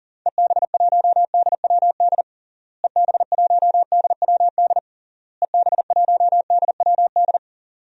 Morse code, heard as a keyed tone, about 40 wpm, 700 Hz. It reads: EB1DWD EB1DWD EB1DWD
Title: DXCC-100 Call Signs - 6 Characters - Repeated 3x - 40wpm